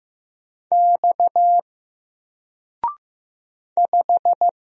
Morse code, heard as a keyed tone, about 15 wpm, 700 Hz. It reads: X 5
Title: Single Letter-Number Alt 15wpm